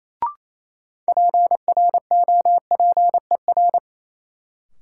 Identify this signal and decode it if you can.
Morse code, heard as a keyed tone, about 28 wpm, 700 Hz. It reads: PROPER
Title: Sentences from Top 800 Words - New Word Drill - 28wpm